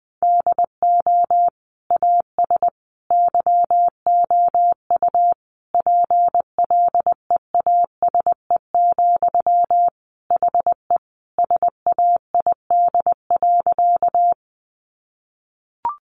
Morse code, heard as a keyed tone, about 20 wpm, 700 Hz. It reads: DO AS YOU PLEASE, HE SAID.
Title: Sentences from Top 1500 Words 20wpm